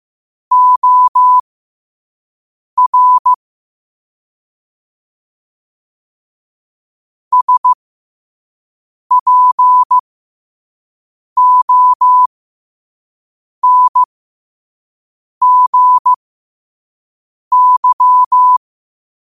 Morse code, heard as a keyed tone, about 15 wpm, spaced as for 5 wpm, 1000 Hz. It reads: OR SPONGY
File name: qotdfiles / 5wpm